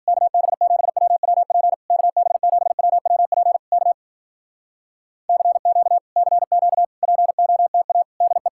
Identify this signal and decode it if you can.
Morse code, heard as a keyed tone, about 60 wpm, 700 Hz. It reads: KB7WWP KB7WWP K CQ CQ POTA DE